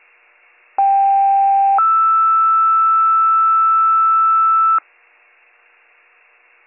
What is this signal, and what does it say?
Two-tone page, heard as a one-second tone, then a three-second tone, about 790 then 1320 Hz. It is heard from 0.8 to 4.8 s.